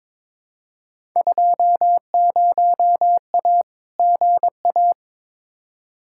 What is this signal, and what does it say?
Morse code, heard as a keyed tone, about 22 wpm, 700 Hz. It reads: 20A GA
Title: Field Day Exchanges Encore - 22wpm